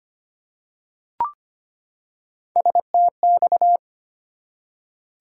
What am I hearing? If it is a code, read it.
Morse code, heard as a keyed tone, about 25 wpm, 700 Hz. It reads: STX
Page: morse-code-ninja-practice-25wpm